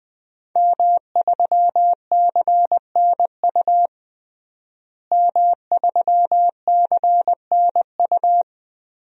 Morse code, heard as a keyed tone, about 20 wpm, 700 Hz. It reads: M3CNU M3CNU
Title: DXCC-100 Call Signs - 5 Characters - Repeated 3x - 20wpm